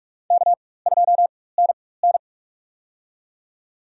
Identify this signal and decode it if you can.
Morse code, heard as a keyed tone, about 46 wpm, spaced as for 21 wpm, 690 Hz. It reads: K2NN
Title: USA Call Signs - 4x Character Spacing - 45wpm